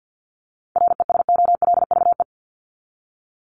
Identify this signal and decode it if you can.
Morse code, heard as a keyed tone, about 49 wpm, 700 Hz. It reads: RESOLVE
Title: Top 1751-2000 Words 50wpm